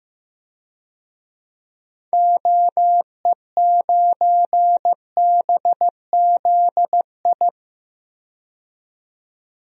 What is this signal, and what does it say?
Morse code, heard as a keyed tone, about 15 wpm, 700 Hz. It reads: OE9BZI
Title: DXCC-100 Call Signs - 6 Characters - 15wpm